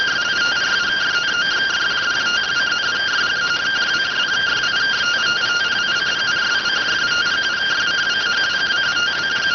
Two-tone teleprinter signal, about 170 Hz shift.